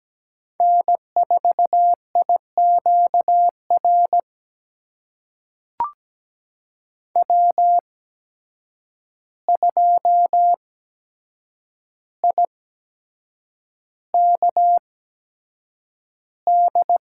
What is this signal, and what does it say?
Morse code, heard as a keyed tone, about 17 wpm, spaced as for 7 wpm, 700 Hz. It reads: N4IQR W2IKD
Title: USA Call Signs - 5 Characters - 8x Character Spacing - 17wpm